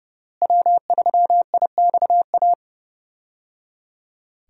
Morse code, heard as a keyed tone, about 28 wpm, 700 Hz. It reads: W3IXA
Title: USA Call Signs - 5 Characters - Repeated 3x - 30wpm